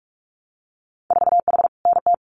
Morse code, heard as a keyed tone, about 46 wpm, 700 Hz. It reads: 4H NT
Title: Winter Field Day Exchanges Encore 45wpm